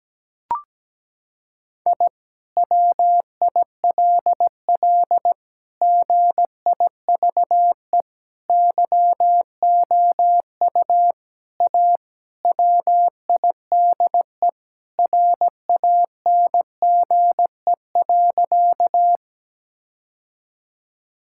Morse code, heard as a keyed tone, about 17 wpm, 700 Hz. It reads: I WILL GIVE YOU A WIDE RANGE.
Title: Sentences from Top 1000 Words - Review Part 6 - 17wpm